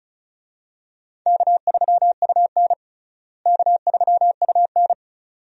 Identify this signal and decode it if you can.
Morse code, heard as a keyed tone, about 36 wpm, 700 Hz. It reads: K3UN K3UN
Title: USA Call Signs - 4 Characters - Repeated 2x - 35wpm